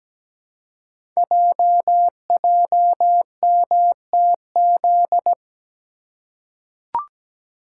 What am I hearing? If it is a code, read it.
Morse code, heard as a keyed tone, about 17 wpm, 700 Hz. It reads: JJMTZ
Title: Code Groups - 5 Letters - 17wpm